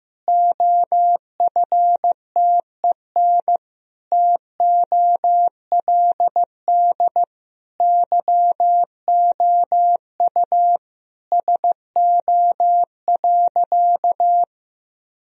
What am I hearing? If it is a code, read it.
Morse code, heard as a keyed tone, about 15 wpm, 700 Hz. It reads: OFTEN TOLD YOU SO.
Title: Sentences from Top 1000 Words - Review Part 1 - 15wpm